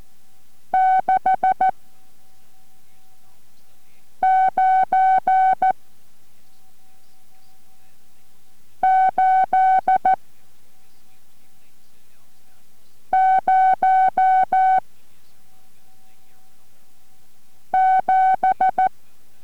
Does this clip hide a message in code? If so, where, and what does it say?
Yes: Morse code from 0.7 to 18.9 s, heard as a keyed tone, about 14 wpm, 750 Hz. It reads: 6 9 8 0 7